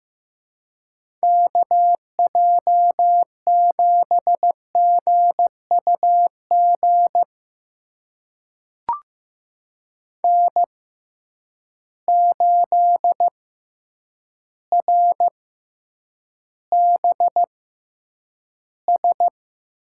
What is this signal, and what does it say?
Morse code, heard as a keyed tone, about 15 wpm, spaced as for 8 wpm, 700 Hz. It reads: KJ7GUG N8RBS
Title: USA Call Signs - 6x Character Spacing - 15wpm